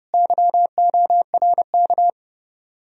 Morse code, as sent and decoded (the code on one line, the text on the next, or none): -.-- --- .-. -.-
YORK